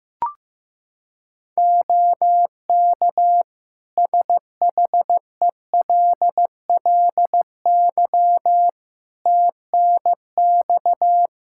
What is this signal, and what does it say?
Morse code, heard as a keyed tone, about 15 wpm, 700 Hz. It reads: OK SHELLY TNX
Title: Practice QSO - Exchange 4 - Ending - 15wpm